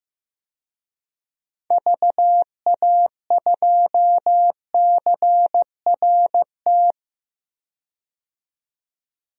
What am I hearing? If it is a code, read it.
Morse code, heard as a keyed tone, about 15 wpm, 700 Hz. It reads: VA2CRT
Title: DXCC-100 Call Signs - 6 Characters - 15wpm